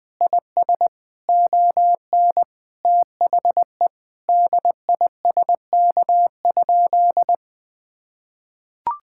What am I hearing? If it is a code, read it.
Morse code, heard as a keyed tone, about 20 wpm, 700 Hz. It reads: IS ON THE DISK?
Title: Sentences from Top 1000 Words - Review Part 6 - 20wpm